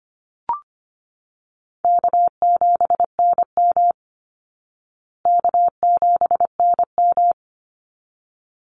Morse code, heard as a keyed tone, about 25 wpm, 700 Hz. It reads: K7NM K7NM
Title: Easy Call Signs Repeated 2x - 25wpm